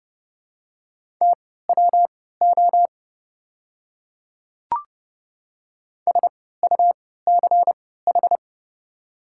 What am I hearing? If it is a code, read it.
Morse code, heard as a keyed tone, about 30 wpm, spaced as for 17 wpm, 700 Hz. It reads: TWO SUCH
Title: 3x Character Spacing - Top 200 Words - 30wpm